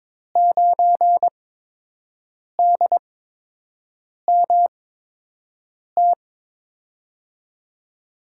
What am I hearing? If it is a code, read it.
Morse code, heard as a keyed tone, about 22 wpm, spaced as for 6 wpm, 700 Hz. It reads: 9DMT
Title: USA Call Signs - 5 Characters - 8x Character Spacing - 22wpm